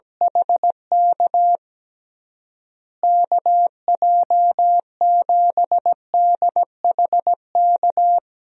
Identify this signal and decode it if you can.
Morse code, heard as a keyed tone, about 17 wpm, 700 Hz. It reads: HK KJ7DHK
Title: USA Call Signs - 6 Characters - Repeated 2x - 17wpm